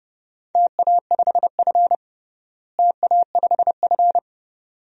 Morse code, heard as a keyed tone, about 32 wpm, 700 Hz. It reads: TA5F TA5F
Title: DXCC-100 Call Signs - 4 Characters - Repeated 2x - 30wpm